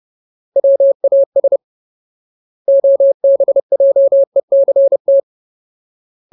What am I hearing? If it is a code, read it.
Morse code, heard as a keyed tone, about 30 wpm, 550 Hz. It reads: WAS OBJECT
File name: Ultimate Word List / 30WPM Ultimate